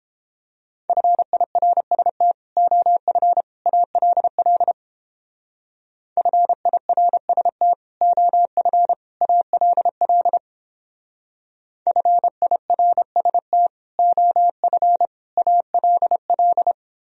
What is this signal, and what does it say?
Morse code, heard as a keyed tone, about 29 wpm, 700 Hz. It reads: FIRST OF ALL FIRST OF ALL FIRST OF ALL